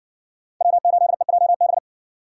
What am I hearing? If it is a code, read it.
Morse code, heard as a keyed tone, about 59 wpm, 700 Hz. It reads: W8EJB